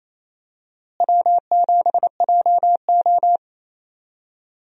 Morse code, heard as a keyed tone, about 28 wpm, 700 Hz. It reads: W7JO
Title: USA Call Signs - 4 Characters - Repeated 2x - 28wpm